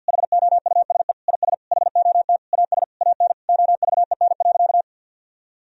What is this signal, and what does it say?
Morse code, heard as a keyed tone, about 50 wpm, 700 Hz. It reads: HOUSE IS HOT AS AN OVEN.
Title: Sentences from Top 1000 Words - Review Part 4 - 50wpm